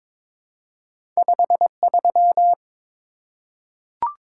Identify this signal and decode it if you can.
Morse code, heard as a keyed tone, about 22 wpm, 700 Hz. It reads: 53
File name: practice / 22wpm / Two Letter-Number Combinations - 22wpm